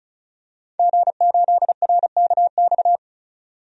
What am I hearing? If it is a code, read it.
Morse code, heard as a keyed tone, about 36 wpm, 700 Hz. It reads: G8RKX
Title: DXCC-100 Call Signs - 5 Characters - 35wpm